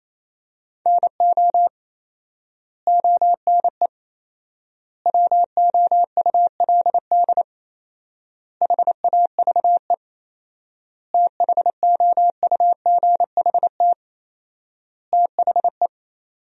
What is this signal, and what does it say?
Morse code, heard as a keyed tone, about 27 wpm, 700 Hz. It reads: NO ONE WOULD HAVE THOUGHT THE
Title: 4x Word-Spacing - Sentences from Top 300 Words - 28wpm